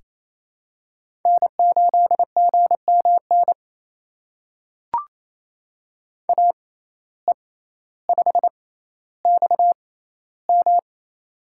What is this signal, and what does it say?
Morse code, heard as a keyed tone, about 27 wpm, spaced as for 14 wpm, 700 Hz. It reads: N8GMN AE5XM
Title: USA Call Signs - 5 Characters - 6x Character Spacing - 28wpm